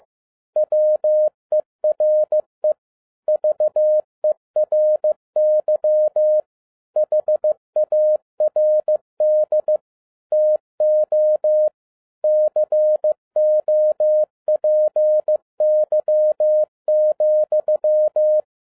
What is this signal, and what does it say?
Morse code, heard as a keyed tone, about 15 wpm, 610 Hz. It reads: WERE VERY HARD TO COPY,
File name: MY CW LIFE 1 15wpm